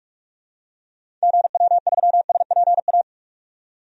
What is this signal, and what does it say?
Morse code, heard as a keyed tone, about 46 wpm, 700 Hz. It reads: GW3SPA
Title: DXCC-100 Call Signs - 6 Characters - 45wpm